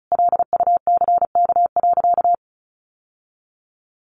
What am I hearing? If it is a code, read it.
Morse code, heard as a keyed tone, about 34 wpm, 700 Hz. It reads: LUCK.
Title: Sentences from Top 1500 Words 35wpm